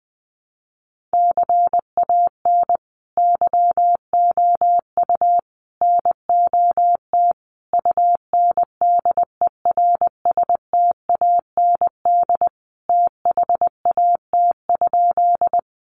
Morse code, heard as a keyed tone, about 20 wpm, 700 Hz. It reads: CAN YOU NOT UNDERSTAND THAT?
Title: Sentences from Top 600 Words 20wpm